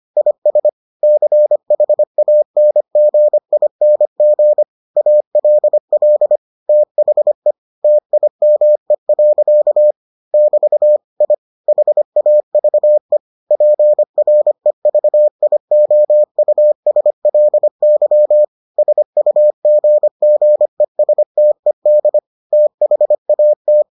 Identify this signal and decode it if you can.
Morse code, heard as a keyed tone, about 25 wpm, 600 Hz. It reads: IS CHANGING ALL THE TIME. <BT> I HAVE PREVIOUSLY SUGGESTED THAT